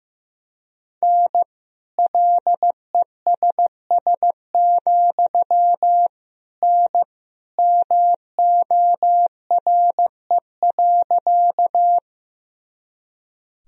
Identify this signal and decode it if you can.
Morse code, heard as a keyed tone, about 15 wpm, 700 Hz. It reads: N LESS, N MORE.
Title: Sentences from Top 400 Words and Abbreviations 15wpm